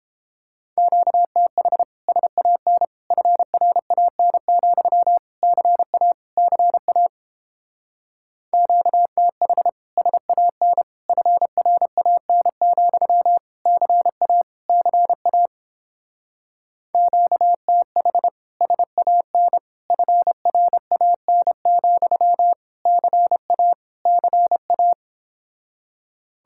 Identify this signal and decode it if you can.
Morse code, heard as a keyed tone, about 29 wpm, 700 Hz. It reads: QTH SAN FRAN, CA CA QTH SAN FRAN, CA CA QTH SAN FRAN, CA CA